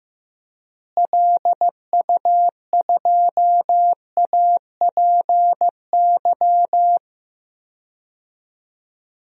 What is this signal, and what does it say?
Morse code, heard as a keyed tone, about 15 wpm, 700 Hz. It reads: LU2APY
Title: DXCC-100 Call Signs - 6 Characters - Repeated 2x - 15wpm